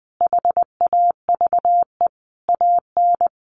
Morse code, heard as a keyed tone, about 20 wpm, 700 Hz. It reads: HAVE AN
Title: Sentences from Top 1300 Words 20wpm